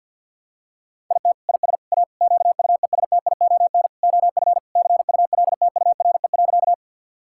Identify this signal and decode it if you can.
Morse code, heard as a keyed tone, about 52 wpm, 700 Hz. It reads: IT IS A QUESTION OF CULTURE.
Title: Sentences from Top 1000 Words - Review Part 6 - 50wpm